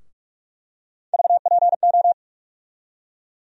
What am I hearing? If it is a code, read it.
Morse code, heard as a keyed tone, about 46 wpm, 700 Hz. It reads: VPO
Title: USA Call Sign Postfixes - 3 Letters 45wpm